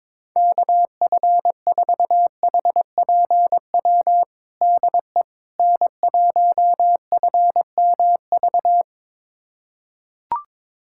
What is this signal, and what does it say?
Morse code, heard as a keyed tone, about 22 wpm, 700 Hz. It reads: KF4HPW DE N1FMV
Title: USA Callsign DE Callsign 22wpm